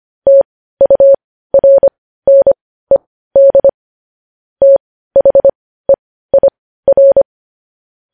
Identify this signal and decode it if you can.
Morse code, heard as a keyed tone, about 25 wpm, spaced as for 15 wpm, 550 Hz. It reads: TURNED THEIR
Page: news headlines in morse code at 15 wpm